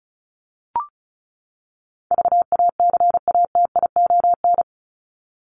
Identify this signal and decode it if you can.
Morse code, heard as a keyed tone, about 34 wpm, 700 Hz. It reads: VACATION
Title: Top 1501-1750 Words 35wpm